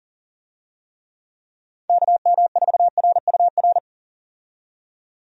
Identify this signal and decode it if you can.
Morse code, heard as a keyed tone, about 37 wpm, 700 Hz. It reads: KM4RUR
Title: USA Call Signs - 6 Characters - 40wpm